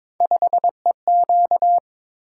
Morse code, heard as a keyed tone, about 22 wpm, 700 Hz. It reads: 5EQ
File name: practice / 22wpm / Three Letter-Number Combinations - 22wpm